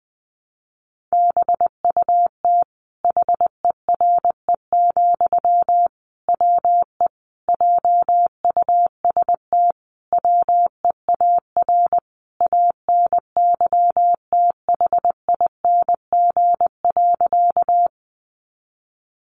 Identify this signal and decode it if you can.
Morse code, heard as a keyed tone, about 19 wpm, 700 Hz. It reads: BUT HERE, WE JUST WEAR ANYTHING.
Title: Sentences from Top 1500 Words 20wpm